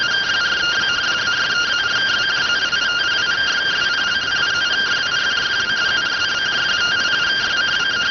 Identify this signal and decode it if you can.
Two-tone teleprinter signal, about 170 Hz shift.